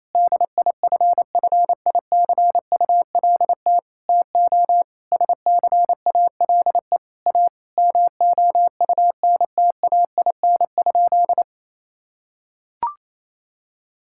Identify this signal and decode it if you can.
Morse code, heard as a keyed tone, about 28 wpm, 700 Hz. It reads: DIFFICULT TO SCALE A MOUNTAIN?
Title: Sentences from Top 1000 Words - Review Part 5 - 28wpm